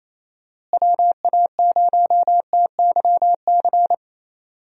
Morse code, heard as a keyed tone, about 28 wpm, 700 Hz. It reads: WA0TYC
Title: USA Call Signs - 6 Characters - Repeated 3x - 28wpm